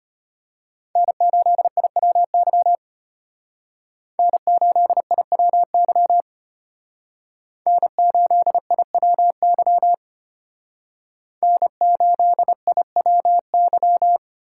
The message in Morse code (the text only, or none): N8IWY N8IWY N8IWY N8IWY